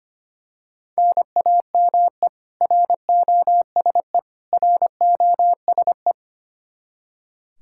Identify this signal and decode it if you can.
Morse code, heard as a keyed tone, about 25 wpm, 700 Hz. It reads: NAME ROSE ROSE
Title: Practice QSO - Exchange 1 - Beginning - 25wpm